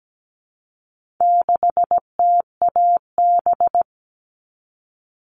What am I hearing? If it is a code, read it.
Morse code, heard as a keyed tone, about 17 wpm, 700 Hz. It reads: 6TAB
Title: All Cut Numbers - 4-Digit 17wpm